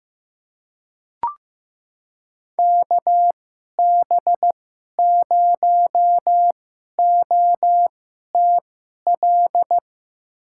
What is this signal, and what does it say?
Morse code, heard as a keyed tone, about 15 wpm, spaced as for 11 wpm, 700 Hz. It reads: KB0OTL
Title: USA Call Signs - 6 Characters - 2x Character Spacing - 15wpm